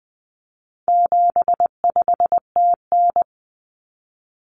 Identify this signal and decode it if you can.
Morse code, heard as a keyed tone, about 20 wpm, 700 Hz. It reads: 75TN
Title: Common Cut Numbers - 4-Digit 20wpm